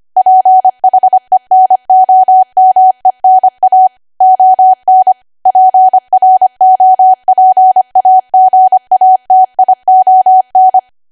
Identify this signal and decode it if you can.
Morse code, heard as a keyed tone, about 25 wpm, 750 Hz. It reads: PHENOMENA ON PROPAGATION